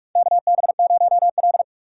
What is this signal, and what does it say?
Morse code, heard as a keyed tone, about 44 wpm, 700 Hz. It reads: KB0L
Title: USA Call Signs - 4 Characters - 2x Character Spacing - 45wpm